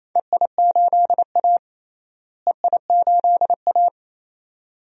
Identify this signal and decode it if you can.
Morse code, heard as a keyed tone, about 28 wpm, 700 Hz. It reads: EI8A EI8A
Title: DXCC-100 Call Signs - 4 Characters - Repeated 2x - 28wpm